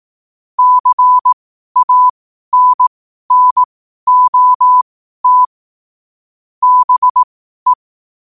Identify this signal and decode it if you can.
Morse code, heard as a keyed tone, about 18 wpm, spaced as for 12 wpm, 1000 Hz. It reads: CANNOT BE
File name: qotdfiles / 12wpm